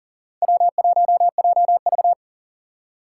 Morse code, heard as a keyed tone, about 40 wpm, 700 Hz. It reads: W1JV
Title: USA Call Signs - 4 Characters - 2x Character Spacing - 40wpm